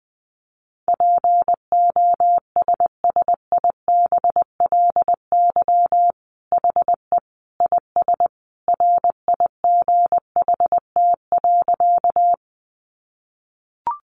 Morse code, heard as a keyed tone, about 20 wpm, 700 Hz. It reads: POSSIBLY HE IS RIGHT.
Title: Sentences from Top 1300 Words 20wpm